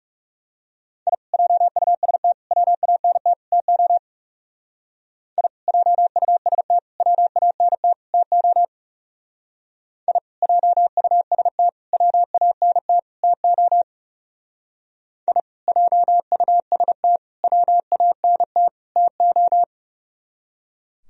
Morse code, heard as a keyed tone, about 38 wpm, 700 Hz. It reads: I JUST WANT TO I JUST WANT TO I JUST WANT TO I JUST WANT TO